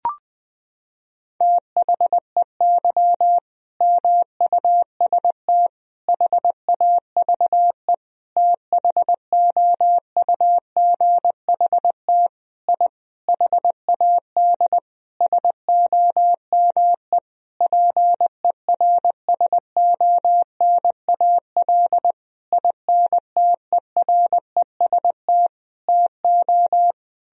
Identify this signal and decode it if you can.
Morse code, heard as a keyed tone, about 21 wpm, 700 Hz. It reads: THEY MUST HAVE THOUGHT I HAD SOME PERSONAL INTEREST TO
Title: Sentences from Top 1000 Words - Review Part 4 - 20wpm